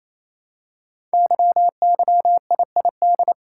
Morse code, heard as a keyed tone, about 27 wpm, 700 Hz. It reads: YYIID
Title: Code Groups - 5 Letter:Numbers - 28wpm